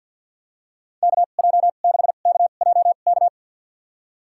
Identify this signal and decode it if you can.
Morse code, heard as a keyed tone, about 53 wpm, spaced as for 37 wpm, 700 Hz. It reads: KJ6KJK
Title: USA Call Signs - 2x Character Spacing - 50wpm